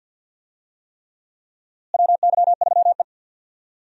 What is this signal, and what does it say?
Morse code, heard as a keyed tone, about 46 wpm, 700 Hz. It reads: WY3E
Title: USA Call Signs - 4 Characters - Repeated 3x - 50wpm